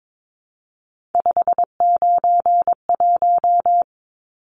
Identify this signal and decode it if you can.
Morse code, heard as a keyed tone, about 22 wpm, 700 Hz. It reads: 591
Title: Numbers - 3-Digit 22wpm